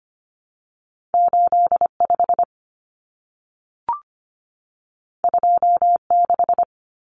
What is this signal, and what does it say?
Morse code, heard as a keyed tone, about 25 wpm, 700 Hz. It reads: 85 26
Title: Numbers - 2-Digit 25wpm